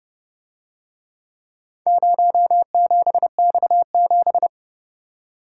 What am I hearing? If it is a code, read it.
Morse code, heard as a keyed tone, about 29 wpm, 700 Hz. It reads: 07X7